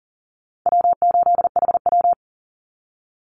Morse code, heard as a keyed tone, about 40 wpm, 700 Hz. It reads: W8HW
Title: Easy Call Signs Repeated 2x - 40wpm